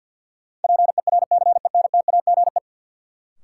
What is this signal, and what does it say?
Morse code, heard as a keyed tone, about 52 wpm, 700 Hz. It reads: PERCENTAGE
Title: Sentences from Top 1300 Words - New Word Drill - 50wpm